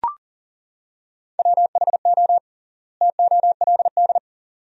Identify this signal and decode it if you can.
Morse code, heard as a keyed tone, about 40 wpm, 700 Hz. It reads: WHO TOLD
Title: 3x Word-Spacing - Sentences from Top 400 Words - 40wpm